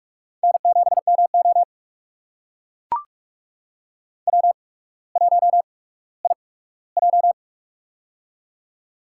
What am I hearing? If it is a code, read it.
Morse code, heard as a keyed tone, about 44 wpm, spaced as for 19 wpm, 700 Hz. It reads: N7MO W1IJ
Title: USA Call Signs - 4 Characters - 8x Character Spacing - 45wpm